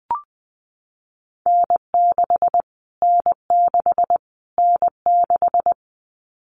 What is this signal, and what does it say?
Morse code, heard as a keyed tone, about 20 wpm, 700 Hz. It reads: N6 N6 N6
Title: Callsign Prefixes for Top 25 DXCC - Repeated 3x - 20wpm